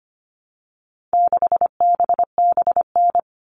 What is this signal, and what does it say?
Morse code, heard as a keyed tone, about 25 wpm, 700 Hz. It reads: 6BBN
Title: All Cut Numbers - 4-Digit 25wpm